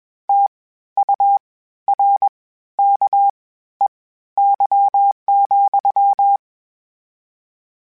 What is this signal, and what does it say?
Morse code, heard as a keyed tone, about 21 wpm, spaced as for 13 wpm, 800 Hz. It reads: TURKEY,
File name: Misc / Misc-mp3s / WordPairs sp @20